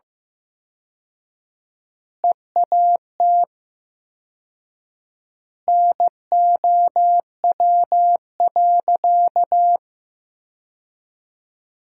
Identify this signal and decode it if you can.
Morse code, heard as a keyed tone, about 15 wpm, 700 Hz. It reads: EAT NOW.
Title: 4x Word-Spacing - Sentences from Top 300 Words - 15wpm